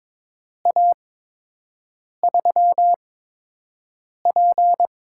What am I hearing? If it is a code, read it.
Morse code, heard as a keyed tone, about 23 wpm, spaced as for 6 wpm, 700 Hz. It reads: A3P
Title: USA Call Signs - 6 Characters - 8x Character Spacing - 22wpm